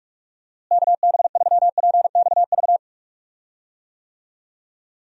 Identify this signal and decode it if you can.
Morse code, heard as a keyed tone, about 47 wpm, 700 Hz. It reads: KB3PXV
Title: USA Call Signs - 6 Characters - Repeated 3x - 45wpm